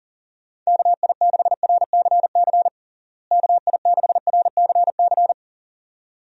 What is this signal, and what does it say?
Morse code, heard as a keyed tone, about 40 wpm, 700 Hz. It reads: KI6RCC KI6RCC
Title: USA Call Signs - 6 Characters - Repeated 2x - 40wpm